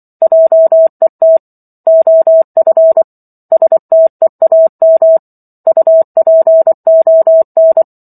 Morse code, heard as a keyed tone, about 24 wpm, 650 Hz. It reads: JET OF STEAM UPON